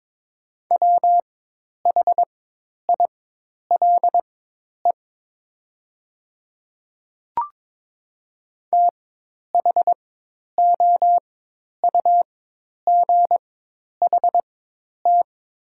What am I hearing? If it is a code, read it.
Morse code, heard as a keyed tone, about 22 wpm, spaced as for 10 wpm, 700 Hz. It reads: WHILE THOUGHT
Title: 4x Character Spacing - Top 300 Words - 22wpm